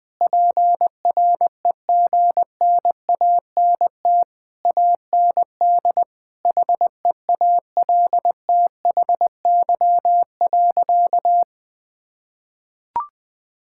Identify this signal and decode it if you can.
Morse code, heard as a keyed tone, about 21 wpm, 700 Hz. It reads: PREGNANT AND HEALTHY.
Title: Sentences from Top 1000 Words - Review Part 6 - 20wpm